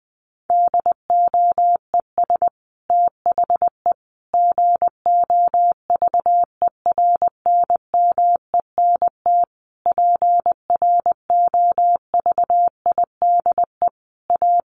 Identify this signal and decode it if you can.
Morse code, heard as a keyed tone, about 19 wpm, 700 Hz. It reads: DOES THE GOVERNMENT PROVIDE A
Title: Sentences from Top 1500 Words 20wpm